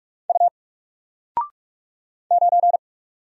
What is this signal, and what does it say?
Morse code, heard as a keyed tone, about 44 wpm, 700 Hz. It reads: U 9
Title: Single Letter-Number Alt 45wpm